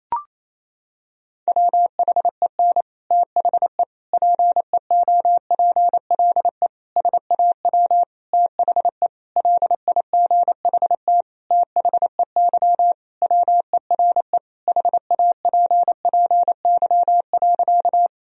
Morse code, heard as a keyed tone, about 28 wpm, 700 Hz. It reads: WHEN THE PEOPLE SAW THE LIGHT THEY WERE HAPPY.
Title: Sentences from Top 1000 Words - Review Part 1 - 28wpm